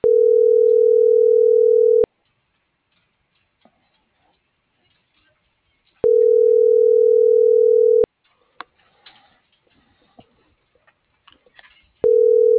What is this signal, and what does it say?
Ringback tone (North America), 440 + 480 Hz.